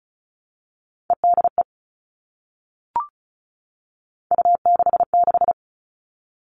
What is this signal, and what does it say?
Morse code, heard as a keyed tone, about 34 wpm, 700 Hz. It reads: EDE U66
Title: All Cut Numbers - 3-Digit 35wpm